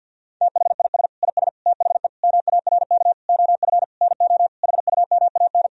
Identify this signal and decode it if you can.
Morse code, heard as a keyed tone, about 46 wpm, 700 Hz. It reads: THIS IS THE MARK OF NO HUMAN
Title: Sentences from Top 1000 Words - Review Part 3 - 50wpm